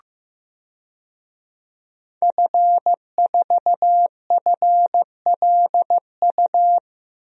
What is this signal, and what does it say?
Morse code, heard as a keyed tone, about 16 wpm, 700 Hz. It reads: F4FLU
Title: DXCC-100 Call Signs - 5 Characters - Repeated 3x - 15wpm